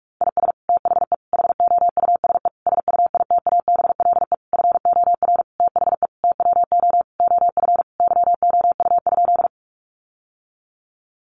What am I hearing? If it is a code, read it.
Morse code, heard as a keyed tone, about 44 wpm, 700 Hz. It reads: IS THE HOUSE SUITABLE FOR THE TWO OF YOU?
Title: Sentences from Top 1400 Words 45wpm